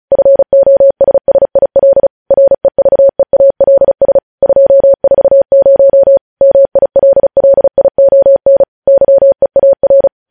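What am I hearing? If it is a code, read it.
Morse code, heard as a keyed tone, about 36 wpm, 550 Hz. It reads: FOSSIL REVEALS 240 MILLION YEAR
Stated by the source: Morse code transmission of news headlines, preceded by vvv.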